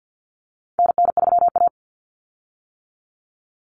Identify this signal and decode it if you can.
Morse code, heard as a keyed tone, about 50 wpm, 700 Hz. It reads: NN3A
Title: Common Cut Numbers - 4-Digit 50wpm